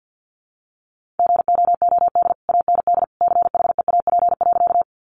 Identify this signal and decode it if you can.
Morse code, heard as a keyed tone, about 50 wpm, 700 Hz. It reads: GOOD AND CHEAP.
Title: Sentences from Top 1200 Words 50wpm